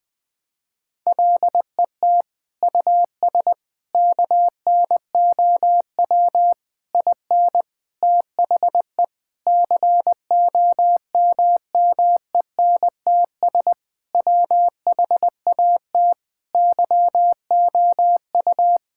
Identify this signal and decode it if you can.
Morse code, heard as a keyed tone, about 20 wpm, 700 Hz. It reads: LET US KNOW IN THE COMMENTS WHAT YOU
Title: Sentences with 6-Word Phrases - 20wpm